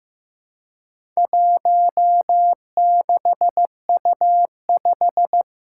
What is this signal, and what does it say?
Morse code, heard as a keyed tone, about 15 wpm, 700 Hz. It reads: 16U5
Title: Four Challenging Letter-Number Combinations 15wpm